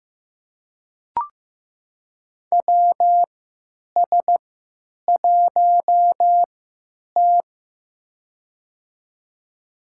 Morse code, heard as a keyed tone, about 15 wpm, spaced as for 9 wpm, 700 Hz. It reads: WS1T
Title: USA Call Signs - 3x Character Spacing - 15wpm